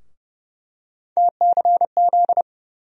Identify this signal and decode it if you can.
Morse code, heard as a keyed tone, about 30 wpm, 700 Hz. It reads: TCZ
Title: USA Call Sign Postfixes - 3 Letters 30wpm